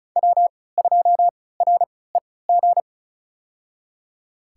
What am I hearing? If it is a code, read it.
Morse code, heard as a keyed tone, about 33 wpm, spaced as for 19 wpm, 700 Hz. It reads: W2REG
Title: USA Call Signs - 3x Character Spacing - 35wpm